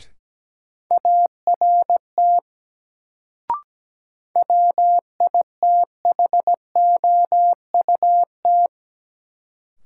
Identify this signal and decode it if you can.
Morse code, heard as a keyed tone, about 17 wpm, 700 Hz. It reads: ART WITHOUT
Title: Sentences from Top 600 Words - New Word Drill - 17wpm